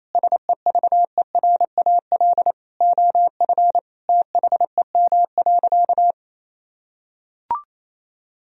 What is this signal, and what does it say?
Morse code, heard as a keyed tone, about 28 wpm, 700 Hz. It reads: SEVERAL OF THEM.
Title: Sentences from Top 1000 Words - Review Part 2 - 28wpm